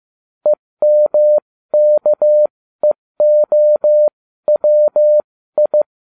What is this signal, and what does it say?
Morse code, heard as a keyed tone, about 15 wpm, spaced as for 13 wpm, 600 Hz. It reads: EMKEOWI